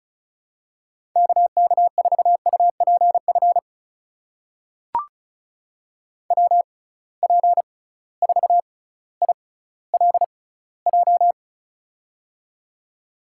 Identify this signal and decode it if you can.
Morse code, heard as a keyed tone, about 36 wpm, spaced as for 18 wpm, 700 Hz. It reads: KK4UPF WP4ILJ
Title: USA Call Signs - 6x Character Spacing - 35wpm